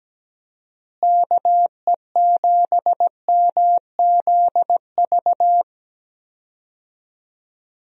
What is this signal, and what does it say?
Morse code, heard as a keyed tone, about 17 wpm, 700 Hz. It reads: KE7MZV
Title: USA Call Signs - 6 Characters - Repeated 2x - 17wpm